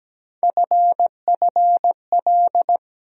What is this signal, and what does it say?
Morse code, heard as a keyed tone, about 17 wpm, 700 Hz. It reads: FFL
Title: USA Call Sign Postfixes - 3 Letters 17wpm